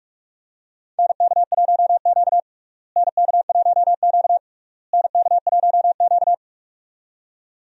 Morse code, heard as a keyed tone, about 46 wpm, 700 Hz. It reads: NK1Q NK1Q NK1Q
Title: USA Call Signs - 4 Characters - Repeated 3x - 45wpm